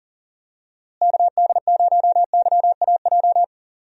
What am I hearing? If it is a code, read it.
Morse code, heard as a keyed tone, about 40 wpm, 700 Hz. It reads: KD0YAJ
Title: USA Call Signs - 6 Characters - Repeated 3x - 40wpm